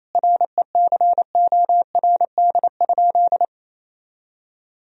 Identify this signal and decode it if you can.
Morse code, heard as a keyed tone, about 27 wpm, 700 Hz. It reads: RECORD?
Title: Sentences from Top 1000 Words - Review Part 3 - 28wpm